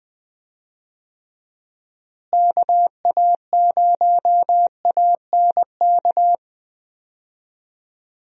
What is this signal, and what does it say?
Morse code, heard as a keyed tone, about 20 wpm, 700 Hz. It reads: KA0ANK
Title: USA Call Signs - 6 Characters - Repeated 3x - 20wpm